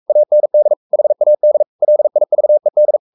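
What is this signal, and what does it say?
Morse code, heard as a keyed tone, about 44 wpm, 600 Hz. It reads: AND HAD LIVED
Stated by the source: Morse code at 43 wpm sending Ken Follett's novel Never in English, from CW med Gnister.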